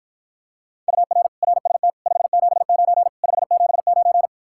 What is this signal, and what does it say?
Morse code, heard as a keyed tone, about 53 wpm, 700 Hz. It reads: UR RST 579 579